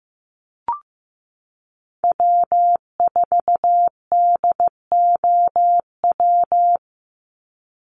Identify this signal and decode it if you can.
Morse code, heard as a keyed tone, about 15 wpm, 700 Hz. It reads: W4DOW
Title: NA SOTA Chaser Callsigns 15wpm